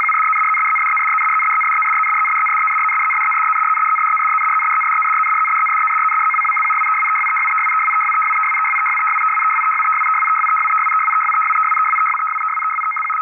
RTTY, about 75 baud. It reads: RYRYRYRYRYRYRYRYRYRYRYRYRYRYRYRYRYRYRYRYRYRYRYRYRYRYRYRYRY SGSGSGSGSGSGSGSGSGSGSGSGSGSGSGSGSGSGSGSGSGSGSGSGSGSGSGSGSG
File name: py1vhf / 30nov2008 PWZ-33 test tape RTTY 850 75R 6449 2134z